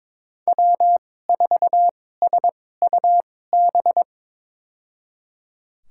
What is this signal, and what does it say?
Morse code, heard as a keyed tone, about 22 wpm, spaced as for 16 wpm, 700 Hz. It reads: W4SUB
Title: USA Call Signs - 5 Characters - 2x Character Spacing - 22wpm